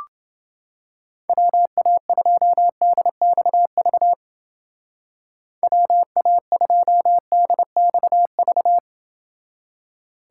Morse code, heard as a keyed tone, about 28 wpm, 700 Hz. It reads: WA2DXV WA2DXV